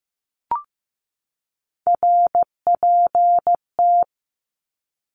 Morse code, heard as a keyed tone, about 15 wpm, 700 Hz. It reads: RPT
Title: Abbreviations 15wpm